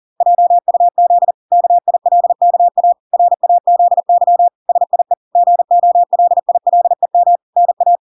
Morse code, heard as a keyed tone, about 40 wpm, 700 Hz. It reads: JUZ KILKA RAZY SIE GOLILEM NA